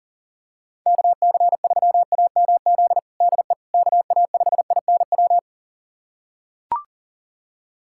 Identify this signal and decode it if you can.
Morse code, heard as a keyed tone, about 37 wpm, 700 Hz. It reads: KC3AMZ DE KA5INW
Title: USA Callsign DE Callsign - 6 Characters - 40wpm